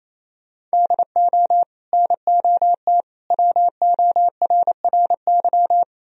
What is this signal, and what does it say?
Morse code, heard as a keyed tone, about 38 wpm, spaced as for 33 wpm, 700 Hz. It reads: DO NOT WORRY
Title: Speed-Racing - Sets of 5 Words 28wpm